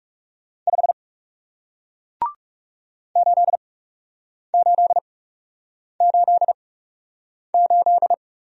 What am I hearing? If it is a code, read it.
Morse code, heard as a keyed tone, about 37 wpm, spaced as for 8 wpm, 700 Hz. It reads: 5 8888
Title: Speed-Racing Single Letter-Number 30wpm